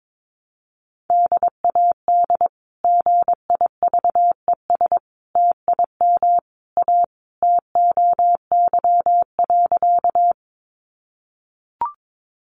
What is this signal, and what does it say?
Morse code, heard as a keyed tone, about 22 wpm, 700 Hz. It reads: DAD GIVES TIM A TOY.
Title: Easy Phrases 22wpm